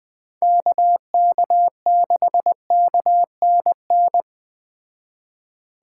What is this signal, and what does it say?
Morse code, heard as a keyed tone, about 20 wpm, 700 Hz. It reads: KK6KNN
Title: Speed Racing - USA 2x3 Call Signs 17wpm